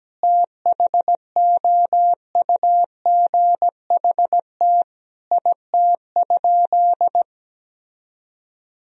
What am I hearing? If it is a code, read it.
Morse code, heard as a keyed tone, about 17 wpm, 700 Hz. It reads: THOUGHT IT?
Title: Sentences from Top 300 Words and Abbreviations 17wpm